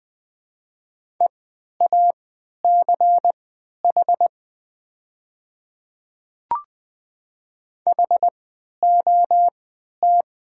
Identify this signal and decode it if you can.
Morse code, heard as a keyed tone, about 20 wpm, spaced as for 11 wpm, 700 Hz. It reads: EACH HOT
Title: 3x Character Spacing - Top 100 Words - 20wpm